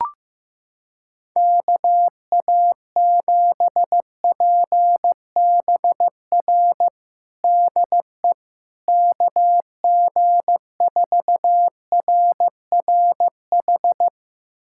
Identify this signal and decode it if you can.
Morse code, heard as a keyed tone, about 15 wpm, 700 Hz. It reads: KA7PBR DE KG4RRH
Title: USA Callsign DE Callsign - 6 Characters - 15wpm